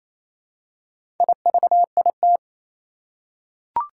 Morse code, heard as a keyed tone, about 28 wpm, 700 Hz. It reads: IVIT
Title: Four Challenging Letter-Number Combinations 28wpm